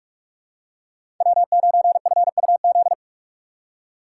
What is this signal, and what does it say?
Morse code, heard as a keyed tone, about 45 wpm, 700 Hz. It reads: W9FUZ
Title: USA Call Signs - 6x Character Spacing - 45wpm